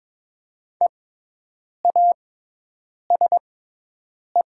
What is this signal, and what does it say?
Morse code, heard as a keyed tone, about 22 wpm, spaced as for 8 wpm, 700 Hz. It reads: EASE
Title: 6x Character Spacing - Top 300 Words - 22wpm